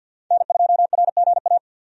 Morse code, heard as a keyed tone, about 48 wpm, 700 Hz. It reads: N2RGA
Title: USA Call Signs - 5 Characters - 50wpm